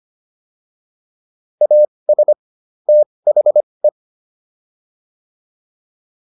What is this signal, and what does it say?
Morse code, heard as a keyed tone, about 25 wpm, spaced as for 20 wpm, 600 Hz. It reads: AS THE